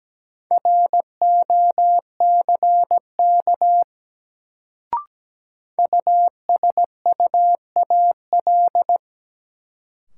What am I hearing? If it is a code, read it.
Morse code, heard as a keyed tone, about 17 wpm, 700 Hz. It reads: ROCK USUAL
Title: Sentences from Top 400 Words - New Word Drill - 17wpm